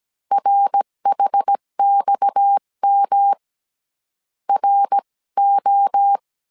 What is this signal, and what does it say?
Morse code, heard as a keyed tone, about 17 wpm, 800 Hz. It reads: RHXM RO
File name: CW / DET / 1 6 1500C S C 16wpm 2733